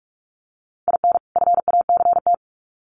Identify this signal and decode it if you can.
Morse code, heard as a keyed tone, about 44 wpm, 700 Hz. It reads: IN FACT
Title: Sets of 2 Words 45wpm